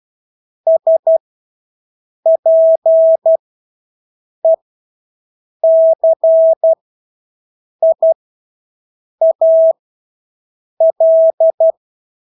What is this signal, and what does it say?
Morse code, heard as a keyed tone, about 12 wpm, spaced as for 6 wpm, 650 Hz. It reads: SPECIAL
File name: Engelske ord 12-6 wpm